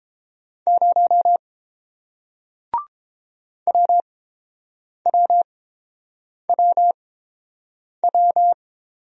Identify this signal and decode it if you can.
Morse code, heard as a keyed tone, about 30 wpm, spaced as for 7 wpm, 700 Hz. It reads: 0 WWWW